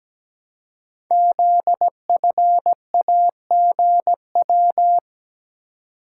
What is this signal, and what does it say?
Morse code, heard as a keyed tone, about 17 wpm, 700 Hz. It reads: ZFAGW
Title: Code Groups - 5 Letters - 17wpm